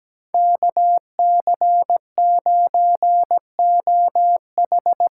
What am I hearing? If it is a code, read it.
Morse code, heard as a keyed tone, about 17 wpm, 700 Hz. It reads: KC9OH
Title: USA Call Signs - 6 Characters - 8x Character Spacing - 17wpm